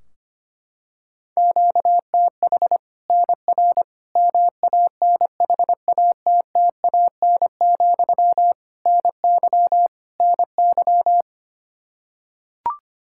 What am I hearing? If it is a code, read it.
Morse code, heard as a keyed tone, about 25 wpm, 700 Hz. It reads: QTH NR MANHATTAN, NY NY
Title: Practice QSO - Exchange 1 - Beginning - 25wpm